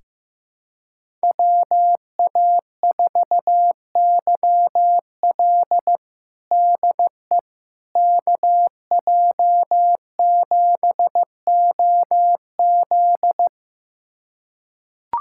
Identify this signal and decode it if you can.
Morse code, heard as a keyed tone, about 15 wpm, 700 Hz. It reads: WA4YL DE KJ7OZ
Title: USA Callsign DE Callsign 15wpm